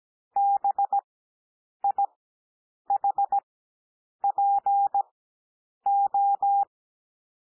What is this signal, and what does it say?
Morse code, heard as a keyed tone, about 17 wpm, spaced as for 8 wpm, 800 Hz. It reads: BIHPO
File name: cwmp3 / L2 SOUND LETTERS 17 wpm